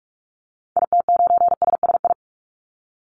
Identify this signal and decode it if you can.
Morse code, heard as a keyed tone, about 44 wpm, 700 Hz. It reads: IT9SSI
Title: Call Signs Repeated 2x - 45wpm